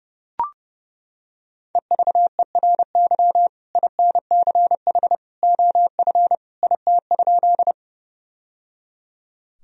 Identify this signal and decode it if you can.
Morse code, heard as a keyed tone, about 30 wpm, 700 Hz. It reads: EVERY INCH OF IT?
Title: Sentences from Top 1000 Words - Review Part 3 - 30wpm